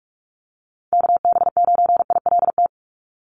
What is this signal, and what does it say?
Morse code, heard as a keyed tone, about 44 wpm, 700 Hz. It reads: KB9ILT
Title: NA SOTA Activator Callsigns 45wpm